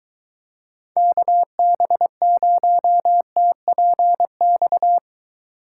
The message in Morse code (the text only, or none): KB0TPX